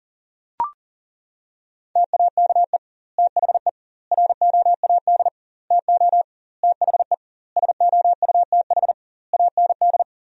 Morse code, heard as a keyed tone, about 40 wpm, 700 Hz. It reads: TAKE THE ROAD TO THE SOUTH AND
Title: 2x Word-Spacing - Sentences from Top 400 Words - 40wpm